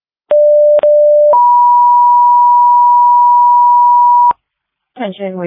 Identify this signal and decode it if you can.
Two-tone page, heard as a one-second tone, then a three-second tone, about 590 then 970 Hz, with noise. It is heard from 0.3 to 4.3 s.